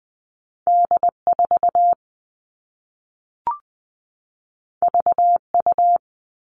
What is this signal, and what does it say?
Morse code, heard as a keyed tone, about 20 wpm, 700 Hz. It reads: D4 VU
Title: All Cut Numbers - 2-Digit 20wpm